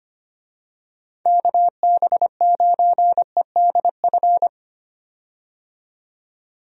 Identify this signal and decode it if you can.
Morse code, heard as a keyed tone, about 25 wpm, 700 Hz. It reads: KB9EDF
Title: USA Call Signs - 6 Characters - 25wpm